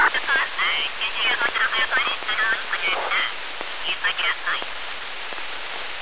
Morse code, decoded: none

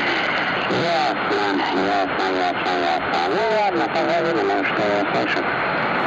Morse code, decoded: none

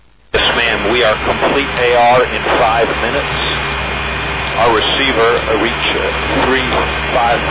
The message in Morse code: none

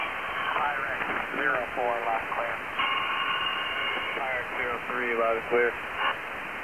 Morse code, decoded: none

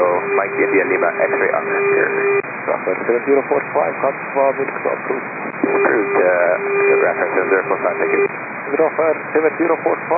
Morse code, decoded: none

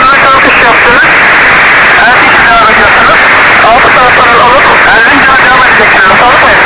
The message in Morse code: none

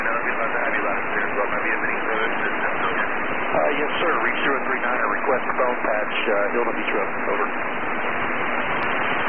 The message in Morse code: none

